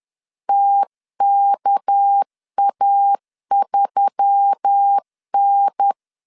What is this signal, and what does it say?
Morse code, heard as a keyed tone, about 10 wpm, 800 Hz. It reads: TKA3N
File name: LEZIONI / 7 GF9Z / 7 G 10wpm 2501